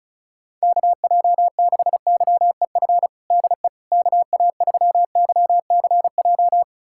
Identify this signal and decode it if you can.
Morse code, heard as a keyed tone, about 34 wpm, 700 Hz. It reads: KJ6YEF DE KA3YCJ
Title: USA Callsign DE Callsign 35wpm